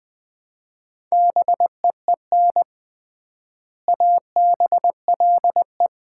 Morse code, heard as a keyed tone, about 20 wpm, 700 Hz. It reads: BEEN ABLE
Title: 3x Word-Spacing - Sets of 3 Words - 20wpm